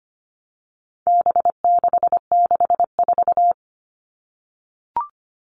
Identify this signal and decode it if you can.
Morse code, heard as a keyed tone, about 25 wpm, 700 Hz. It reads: B664
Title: All Cut Numbers - 4-Digit 25wpm